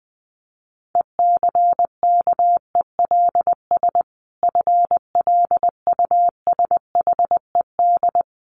Morse code, heard as a keyed tone, about 20 wpm, 700 Hz. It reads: ECKELS FLUSHED
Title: A Sound Of Thunder 20wpm